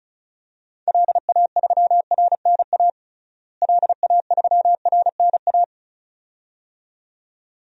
Morse code, heard as a keyed tone, about 34 wpm, 700 Hz. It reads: LA3RNA LA3RNA
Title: DXCC-100 Call Signs - 6 Characters - Repeated 2x - 35wpm